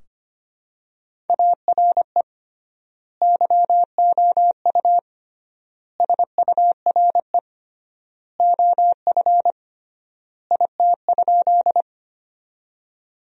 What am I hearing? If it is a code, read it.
Morse code, heard as a keyed tone, about 25 wpm, 700 Hz. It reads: ARE YOU SURE OF IT?
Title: 3x Word-Spacing - Sentences from Top 300 Words - 25wpm